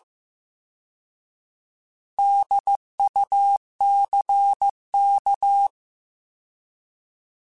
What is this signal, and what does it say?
Morse code, heard as a keyed tone, about 15 wpm, 800 Hz. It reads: DUCK
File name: Alphabetical / Alphabetical mp3s / Ds @13